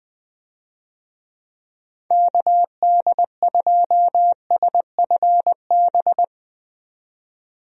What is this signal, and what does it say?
Morse code, heard as a keyed tone, about 21 wpm, 700 Hz. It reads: KD2SFB
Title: USA Call Signs - 6 Characters - 20wpm